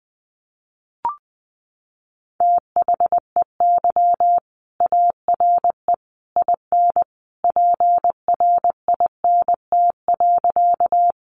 Morse code, heard as a keyed tone, about 20 wpm, 700 Hz. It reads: THEY ARE IN PRINT.
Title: Sentences from Top 1400 Words 20wpm